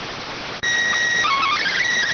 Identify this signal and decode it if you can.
SSTV (slow-scan television), PD 290.